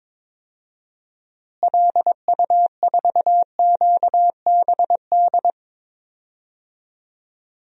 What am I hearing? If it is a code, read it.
Morse code, heard as a keyed tone, about 21 wpm, 700 Hz. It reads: LU4QBD
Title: DXCC-100 Call Signs - 6 Characters - Repeated 3x - 22wpm